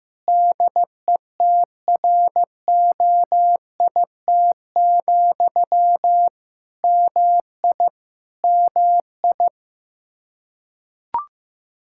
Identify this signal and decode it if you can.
Morse code, heard as a keyed tone, about 15 wpm, 700 Hz. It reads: DETROIT, MI MI
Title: Practice QSO - Exchange 1 - Beginning - 15wpm